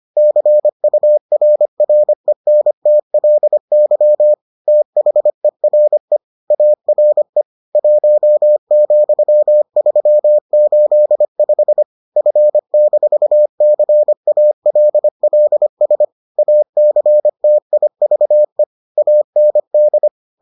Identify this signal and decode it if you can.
Morse code, heard as a keyed tone, about 25 wpm, 600 Hz. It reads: CURRENTLY THERE ARE 1,385 F-CALLS ACTIVE AND